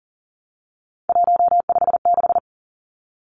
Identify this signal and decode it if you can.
Morse code, heard as a keyed tone, about 40 wpm, 700 Hz. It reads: 156
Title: Numbers - 3-Digit 40wpm